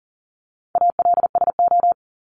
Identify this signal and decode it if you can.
Morse code, heard as a keyed tone, about 40 wpm, 700 Hz. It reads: ALSO
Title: Top 200 Words 40wpm